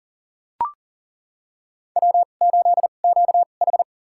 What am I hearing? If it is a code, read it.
Morse code, heard as a keyed tone, about 37 wpm, spaced as for 28 wpm, 700 Hz. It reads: W8QH
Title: USA Call Signs - 4 Characters - 2x Character Spacing - 40wpm